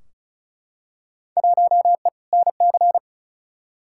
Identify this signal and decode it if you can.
Morse code, heard as a keyed tone, about 34 wpm, 700 Hz. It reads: 1E NC
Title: Field Day Exchanges - 35wpm